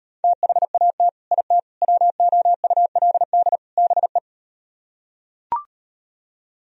Morse code, heard as a keyed tone, about 38 wpm, 700 Hz. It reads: THAT IT WOULD BE